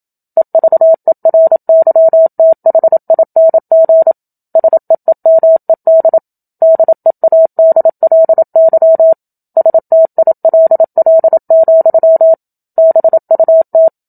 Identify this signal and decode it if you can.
Morse code, heard as a keyed tone, about 27 wpm, 650 Hz. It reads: EVERYTHING SEEMED DEADLY STILL, BUT
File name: Book Two / 01-Chapter 1 - 27 WPM